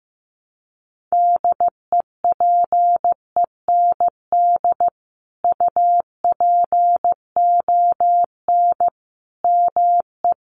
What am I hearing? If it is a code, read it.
Morse code, heard as a keyed tone, about 15 wpm, 700 Hz. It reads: DEPEND UPON ME
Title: Sentences from Top 1300 Words 15wpm